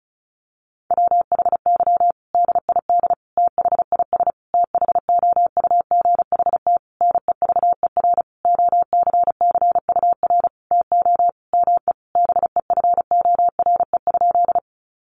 Morse code, heard as a keyed tone, about 34 wpm, 700 Hz. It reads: WHY DID THIS THOUGHT NEVER OCCUR TO ME BEFORE?
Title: Sentences from Top 900 Words 35wpm